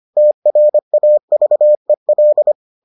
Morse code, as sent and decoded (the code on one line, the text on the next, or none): - .-. .- ...- . .-..
TRAVEL